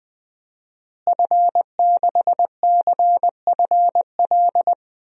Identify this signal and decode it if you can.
Morse code, heard as a keyed tone, about 20 wpm, 700 Hz. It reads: F6CFL
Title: DXCC-100 Call Signs - 5 Characters - Repeated 2x - 20wpm